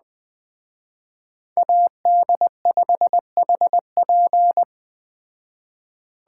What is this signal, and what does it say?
Morse code, heard as a keyed tone, about 20 wpm, 700 Hz. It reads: AD5HP